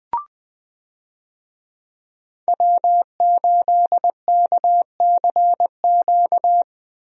Morse code, heard as a keyed tone, about 20 wpm, 700 Hz. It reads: W8KCQ
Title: USA Call Signs - 5 Characters - Repeated 2x - 20wpm